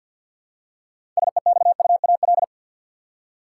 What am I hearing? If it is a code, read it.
Morse code, heard as a keyed tone, about 52 wpm, 700 Hz. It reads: SEXUAL
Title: Six Letter Words - 50wpm